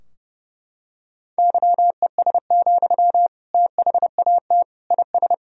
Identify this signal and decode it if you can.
Morse code, heard as a keyed tone, about 30 wpm, 700 Hz. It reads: YES, THAT IS
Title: Sentences from Top 1000 Words - Review Part 3 - 30wpm